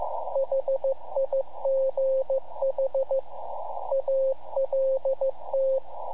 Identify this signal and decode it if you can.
Morse code, heard as a keyed tone, about 15 wpm, 530 Hz, with radio noise. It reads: HIGH ALT
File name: iHAB-3 / Repeater Recordings